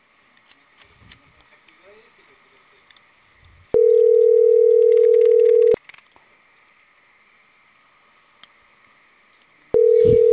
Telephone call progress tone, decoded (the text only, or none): ringback tone